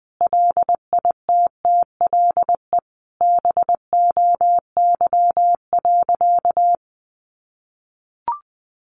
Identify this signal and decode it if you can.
Morse code, heard as a keyed tone, about 20 wpm, 700 Hz. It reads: LITTLE BOY.
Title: Sentences from Top 1000 Words 20wpm